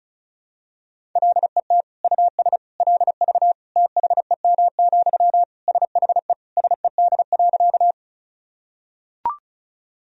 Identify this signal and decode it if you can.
Morse code, heard as a keyed tone, about 36 wpm, 700 Hz. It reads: LET US LV THEM, SHE SED.
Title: Sentences from Top 400 Words and Abbreviations 35wpm